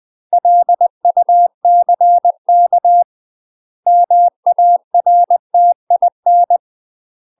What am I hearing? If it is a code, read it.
Morse code, heard as a keyed tone, about 20 wpm, 700 Hz. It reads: LUCK MARTIN
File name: UK MASTER QSOs TRAINING / 20WPM / UK MASTER QSOs 0060